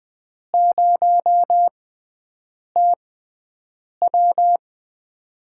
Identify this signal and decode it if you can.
Morse code, heard as a keyed tone, about 20 wpm, 700 Hz. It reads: 0 T W
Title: USA Call Signs - 6x Character Spacing - 20wpm